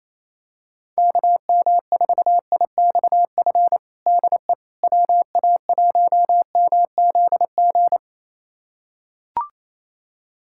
Morse code, heard as a keyed tone, about 28 wpm, 700 Hz. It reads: KM4IXF DE WA1MZG
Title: USA Callsign DE Callsign 28wpm